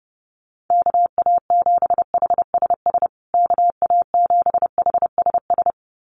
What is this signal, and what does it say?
Morse code, heard as a keyed tone, about 28 wpm, 700 Hz. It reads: KA7HSS KA7HSS
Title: USA Call Signs Repeated 2x 30wpm